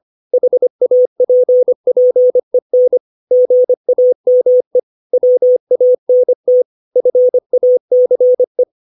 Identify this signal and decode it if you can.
Morse code, heard as a keyed tone, about 25 wpm, 500 Hz. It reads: HAPPEN GAME WANT FACE
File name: 632 words / 632 Words 25WPM